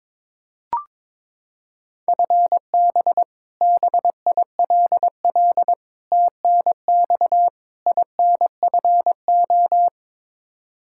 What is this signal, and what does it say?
Morse code, heard as a keyed tone, about 22 wpm, 700 Hz. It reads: FB BILL TNX INFO
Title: Practice QSO - Exchange 3 - More Info - 22wpm